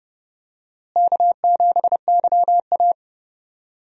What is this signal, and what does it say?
Morse code, heard as a keyed tone, about 31 wpm, 700 Hz. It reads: K7YA
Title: USA Call Signs - 4 Characters - Repeated 2x - 30wpm